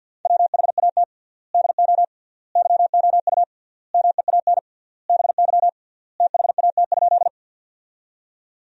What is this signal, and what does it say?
Morse code, heard as a keyed tone, about 50 wpm, 700 Hz. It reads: WHAT DO YOU MEAN BY THAT?
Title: 3x Word-Spacing - Sentences from Top 200 Words - 50wpm